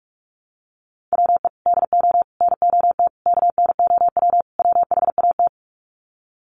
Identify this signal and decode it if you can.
Morse code, heard as a keyed tone, about 44 wpm, 700 Hz. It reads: WE DO NOT KNOW WHAT
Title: Sets of 5 Words 45wpm